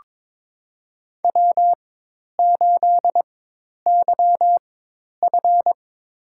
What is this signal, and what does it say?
Morse code, heard as a keyed tone, about 21 wpm, spaced as for 10 wpm, 700 Hz. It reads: W8YF
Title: USA Call Signs - 4 Characters - 4x Character Spacing - 22wpm